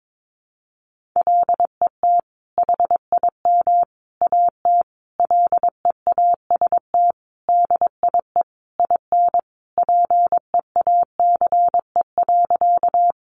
Morse code, heard as a keyed tone, about 22 wpm, 700 Hz. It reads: LET HIM AT LEAST DIE IN PEACE.
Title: Sentences from Top 1500 Words 22wpm